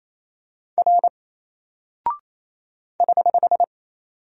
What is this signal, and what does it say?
Morse code, heard as a keyed tone, about 28 wpm, 700 Hz. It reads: R <HH>
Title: Single Character (Letter:Number:Punctuation:Prosign) 28wpm